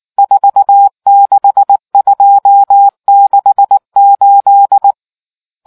Morse code, heard as a keyed tone, about 19 wpm, 800 Hz. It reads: 46268